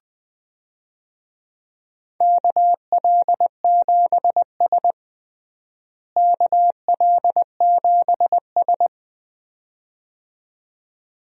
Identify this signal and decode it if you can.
Morse code, heard as a keyed tone, about 21 wpm, 700 Hz. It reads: KL7S KL7S
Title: DXCC-100 Call Signs - 4 Characters - Repeated 2x - 20wpm